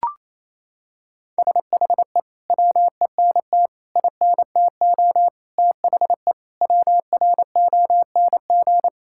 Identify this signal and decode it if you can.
Morse code, heard as a keyed tone, about 27 wpm, 700 Hz. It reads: SHE WENT INTO THE WRONG
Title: Sentences from Top 1600 Words 28wpm